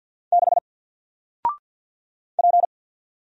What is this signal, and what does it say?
Morse code, heard as a keyed tone, about 50 wpm, 700 Hz. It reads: 6 P
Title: Single Letter-Number 50wpm